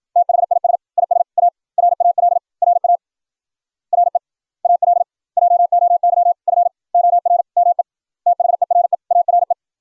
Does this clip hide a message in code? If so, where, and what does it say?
Yes: Morse code from 0.2 to 9.5 s, heard as a keyed tone, about 56 wpm, spaced as for 47 wpm, 700 Hz. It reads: THIS IS A RAL RA REAL JOY FOR ME THERE ARE